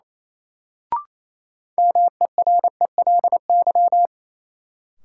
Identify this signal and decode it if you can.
Morse code, heard as a keyed tone, about 27 wpm, 700 Hz. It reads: MERELY
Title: Sentences from Top 1400 Words - New Word Drill - 28wpm